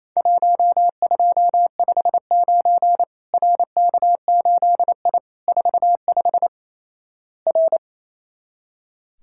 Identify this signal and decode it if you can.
Morse code, heard as a keyed tone, about 28 wpm, 700 Hz. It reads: 1259 RK8I 45 R
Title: Worked All Europe DX Contest - QTC Exchange - 28wpm